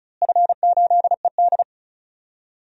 Morse code, heard as a keyed tone, about 36 wpm, 700 Hz. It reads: F8ED
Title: DXCC-100 Call Signs - 4 Characters - Repeated 3x - 35wpm